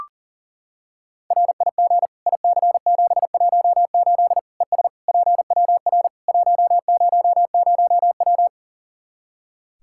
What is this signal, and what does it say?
Morse code, heard as a keyed tone, about 37 wpm, 700 Hz. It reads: RIG IC718 ES PWR 100W
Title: Practice QSO - Exchange 2 - Info - 40wpm